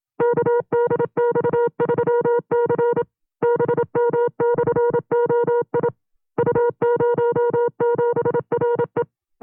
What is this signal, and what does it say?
Morse code, heard as a keyed tone, about 27 wpm, 480 Hz. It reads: KDX3C BM/OI U07RE
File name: CW-practice/25WPM